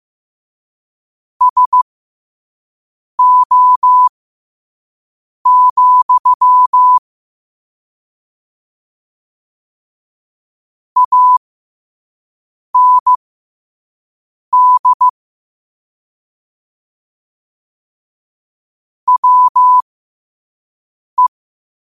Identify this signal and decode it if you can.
Morse code, heard as a keyed tone, about 15 wpm, spaced as for 5 wpm, 1000 Hz. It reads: SO, AND WE
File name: qotdfiles / 5wpm